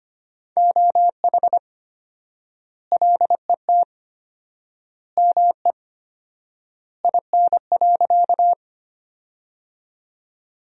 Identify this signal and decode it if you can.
Morse code, heard as a keyed tone, about 25 wpm, 700 Hz. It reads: OH LET ME IN.
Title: 4x Word-Spacing - Sentences from Top 500 Words - 25wpm